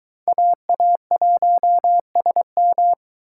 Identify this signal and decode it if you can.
Morse code, heard as a keyed tone, about 23 wpm, 700 Hz. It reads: AA1SM